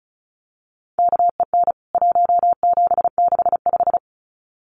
Morse code, heard as a keyed tone, about 34 wpm, 700 Hz. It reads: KEN 1765
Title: CWT Exchange 35wpm